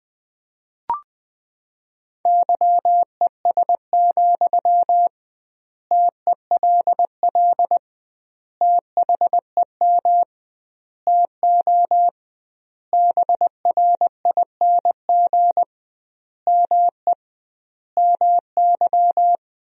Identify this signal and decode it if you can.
Morse code, heard as a keyed tone, about 21 wpm, 700 Hz. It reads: YES, TELL THEM TO BRING ME MY
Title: 2x Word-Spacing - Sentences from Top 500 Words - 20wpm